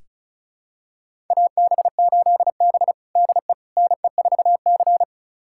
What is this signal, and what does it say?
Morse code, heard as a keyed tone, about 36 wpm, 700 Hz. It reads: AB8B DE NE4C
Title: USA Callsign DE Callsign - 4 Characters - 35wpm